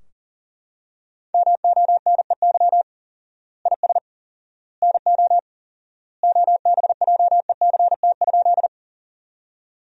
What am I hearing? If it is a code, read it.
Morse code, heard as a keyed tone, about 40 wpm, 700 Hz. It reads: MONEY IS NO OBJECT?
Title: 4x Word-Spacing - Sentences from Top 500 Words - 40wpm